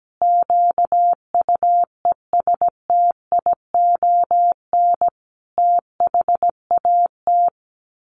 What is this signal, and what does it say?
Morse code, heard as a keyed tone, about 17 wpm, 700 Hz. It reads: QUESTION THAT
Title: Sets of 5 Words Encore 17wpm